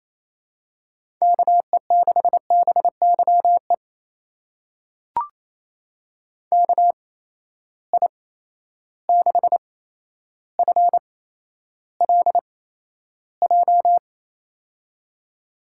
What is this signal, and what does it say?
Morse code, heard as a keyed tone, about 28 wpm, spaced as for 12 wpm, 700 Hz. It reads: KE6BYE KI6FLJ
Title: USA Call Signs - 6 Characters - 8x Character Spacing - 28wpm